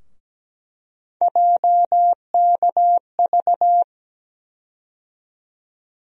Morse code, heard as a keyed tone, about 17 wpm, 700 Hz. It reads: JKV
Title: USA Call Sign Postfixes - 3 Letters 17wpm